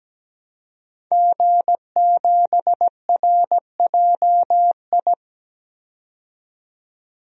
Morse code, heard as a keyed tone, about 17 wpm, 700 Hz. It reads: G7RJI
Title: DXCC-100 Call Signs - 5 Characters - 17wpm